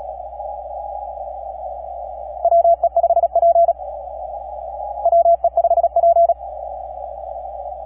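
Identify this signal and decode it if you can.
Morse code, heard as a keyed tone, about 40 wpm, 680 Hz. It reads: WE5P WE5P
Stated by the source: KP5/NP3VI on Desecheo heard in a 15 meter CW pileup.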